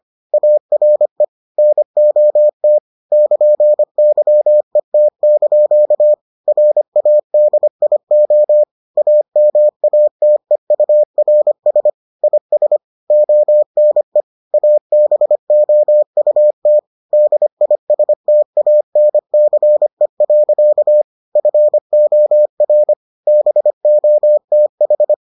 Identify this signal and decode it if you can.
Morse code, heard as a keyed tone, about 25 wpm, 600 Hz. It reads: ARE NOT <KN>YET) RADIO AMATEURS IS ONE ABOUT DISTANCE. FOR BOTH